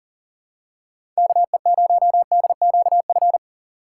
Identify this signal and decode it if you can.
Morse code, heard as a keyed tone, about 40 wpm, 700 Hz. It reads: KE0DQF
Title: USA Call Signs - 6 Characters - 2x Character Spacing - 40wpm